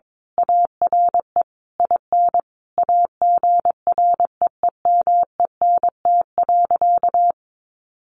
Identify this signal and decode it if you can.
Morse code, heard as a keyed tone, about 22 wpm, 700 Hz. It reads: ARE IN AGREEMENT.
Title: Sentences from Top 1500 Words 22wpm